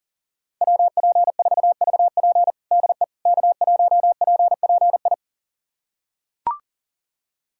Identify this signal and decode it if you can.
Morse code, heard as a keyed tone, about 40 wpm, 700 Hz. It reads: WP4VP DE K1PPI
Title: USA Callsign DE Callsign - 5 Characters - 40wpm